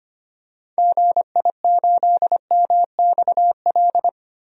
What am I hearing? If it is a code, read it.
Morse code, heard as a keyed tone, about 25 wpm, 700 Hz. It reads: GI8MXL
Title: DXCC-100 Call Signs - 6 Characters - 25wpm